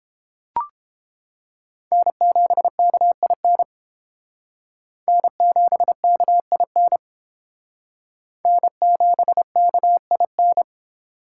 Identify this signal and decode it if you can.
Morse code, heard as a keyed tone, about 30 wpm, 700 Hz. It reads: N7KIN N7KIN N7KIN